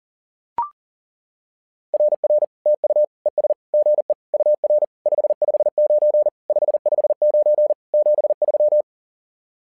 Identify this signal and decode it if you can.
Morse code, heard as a keyed tone, about 40 wpm, 600 Hz. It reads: RR TU ES GE UR 559 559 73
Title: SOTA QSO Practice 40wpm